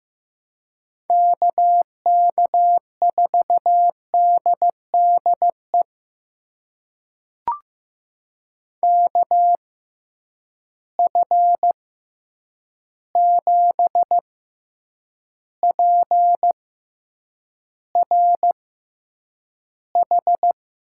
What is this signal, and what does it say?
Morse code, heard as a keyed tone, about 15 wpm, spaced as for 8 wpm, 700 Hz. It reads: KK4DDE KF7PRH
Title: USA Call Signs - 6x Character Spacing - 15wpm